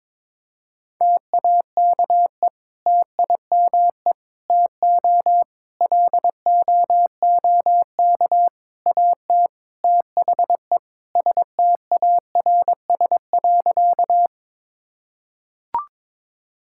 Morse code, heard as a keyed tone, about 22 wpm, 700 Hz. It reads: TAKE TIME TO LOOK AT THE STARS.
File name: practice / 22wpm / Sentences with 4-Word Phrases - 22wpm